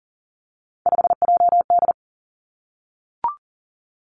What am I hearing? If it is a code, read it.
Morse code, heard as a keyed tone, about 40 wpm, 700 Hz. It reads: 5JD
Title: Three Challenging Letter-Number Combinations 40wpm